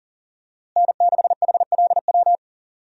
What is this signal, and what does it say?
Morse code, heard as a keyed tone, about 40 wpm, 700 Hz. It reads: N6HLW
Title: USA Call Signs - 5 Characters - Repeated 3x - 40wpm